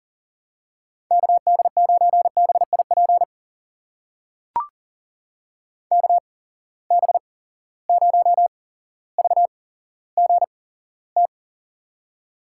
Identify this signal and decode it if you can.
Morse code, heard as a keyed tone, about 38 wpm, spaced as for 17 wpm, 700 Hz. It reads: KD9BIP KB0VGT
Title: USA Call Signs - 6 Characters - 8x Character Spacing - 40wpm